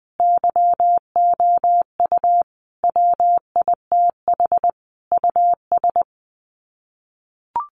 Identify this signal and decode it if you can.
Morse code, heard as a keyed tone, about 20 wpm, 700 Hz. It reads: YOU WITH US
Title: Sets of 5 Words 20wpm